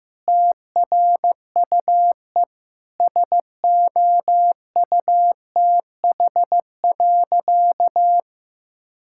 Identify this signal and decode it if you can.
Morse code, heard as a keyed tone, about 16 wpm, 700 Hz. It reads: TRUE SOUTH.
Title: Sentences from Top 1000 Words - Review Part 6 - 15wpm